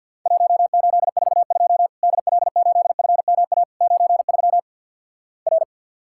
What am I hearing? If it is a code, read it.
Morse code, heard as a keyed tone, about 50 wpm, 700 Hz. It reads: 1842 DL8FMA 93 R
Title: Worked All Europe DX Contest - QTC Exchange - 50wpm